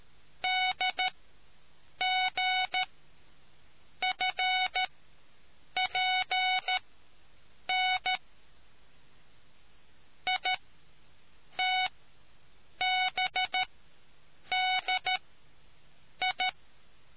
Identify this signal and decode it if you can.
Morse code, heard as a keyed tone, about 13 wpm, spaced as for 7 wpm, 740 Hz. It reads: DGFPN ITBDI